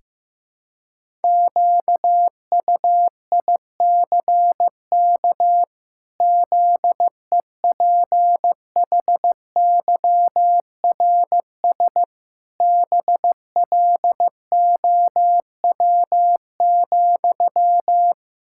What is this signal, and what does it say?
Morse code, heard as a keyed tone, about 15 wpm, 700 Hz. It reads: QUICK ZEPHYRS BLOW,
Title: Pangram Challenge v2 - 15wpm